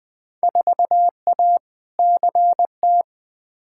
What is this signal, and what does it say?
Morse code, heard as a keyed tone, about 20 wpm, 700 Hz. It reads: 4A CT
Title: Field Day Exchanges Encore - 20wpm